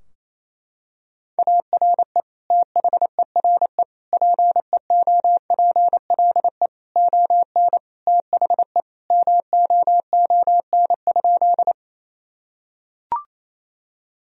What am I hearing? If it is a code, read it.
Morse code, heard as a keyed tone, about 28 wpm, 700 Hz. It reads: ARE THERE PEOPLE ON THE MOON?
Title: Sentences from Top 1000 Words - Review Part 3 - 28wpm